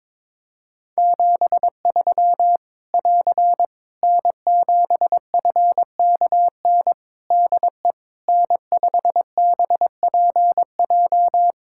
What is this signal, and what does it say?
Morse code, heard as a keyed tone, about 22 wpm, 700 Hz. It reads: 73 <AR> N7FKN DE N5BPJ
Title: Practice QSO - Exchange 4 - Ending - 22wpm